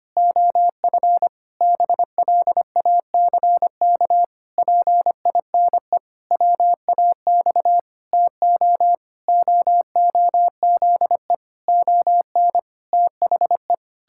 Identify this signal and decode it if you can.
Morse code, heard as a keyed tone, about 25 wpm, 700 Hz. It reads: OF BLACK PINE WAX TO OOZE ON THE
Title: Pangram Challenge v2 - 25wpm